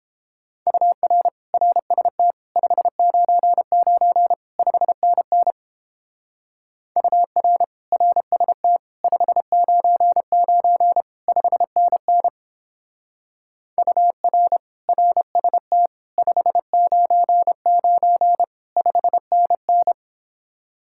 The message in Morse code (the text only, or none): UR RST 599 5NN UR RST 599 5NN UR RST 599 5NN